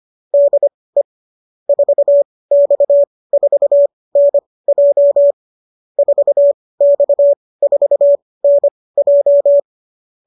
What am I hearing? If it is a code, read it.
Morse code, heard as a keyed tone, about 25 wpm, spaced as for 18 wpm, 590 Hz. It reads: DE 4X4NJ 4X4NJ